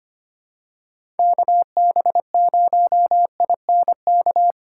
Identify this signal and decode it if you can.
Morse code, heard as a keyed tone, about 25 wpm, 700 Hz. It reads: KB0INK
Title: USA Call Signs - 3x Character Spacing - 25wpm